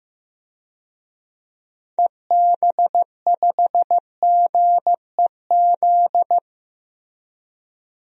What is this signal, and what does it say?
Morse code, heard as a keyed tone, about 15 wpm, 700 Hz. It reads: EB5GEZ
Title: DXCC-100 Call Signs - 6 Characters - Repeated 2x - 15wpm